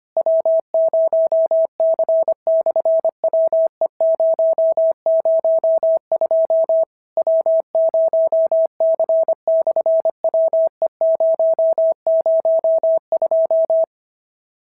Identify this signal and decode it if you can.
Morse code, heard as a keyed tone, about 25 wpm, 650 Hz. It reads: W0C/WE002 W0C/WE002
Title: SOTA QSO Practice 25wpm